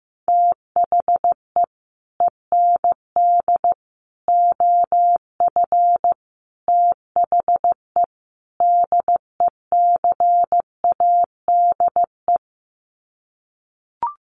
Morse code, heard as a keyed tone, about 15 wpm, 700 Hz. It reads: THE END OF THE DECADE
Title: Sets of 5 Words Encore 15wpm